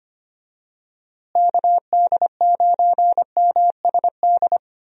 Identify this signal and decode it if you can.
Morse code, heard as a keyed tone, about 25 wpm, 700 Hz. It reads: KD9MSD
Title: USA Call Signs - 6 Characters - Repeated 2x - 25wpm